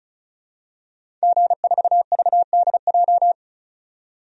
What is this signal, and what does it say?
Morse code, heard as a keyed tone, about 34 wpm, 700 Hz. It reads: G4VDJ
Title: DXCC-100 Call Signs - 5 Characters - 35wpm